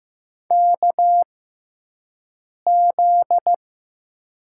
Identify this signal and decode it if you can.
Morse code, heard as a keyed tone, about 15 wpm, spaced as for 5 wpm, 700 Hz. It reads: KZ
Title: USA Call Signs - 4 Characters - 6x Character Spacing - 15wpm